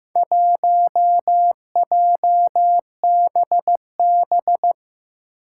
Morse code, heard as a keyed tone, about 15 wpm, 700 Hz. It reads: 1JBB
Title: Four Challenging Letter-Number Combinations 15wpm